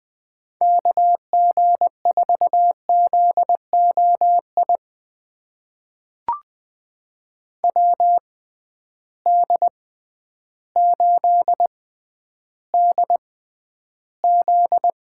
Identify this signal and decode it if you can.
Morse code, heard as a keyed tone, about 20 wpm, spaced as for 11 wpm, 700 Hz. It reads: KG4ZOI WD8DZ
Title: USA Call Signs - 6 Characters - 6x Character Spacing - 20wpm